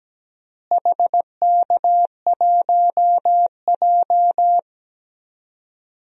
Morse code, heard as a keyed tone, about 17 wpm, 700 Hz. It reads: HK1J
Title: DXCC-100 Call Signs - 4 Characters - Repeated 2x - 17wpm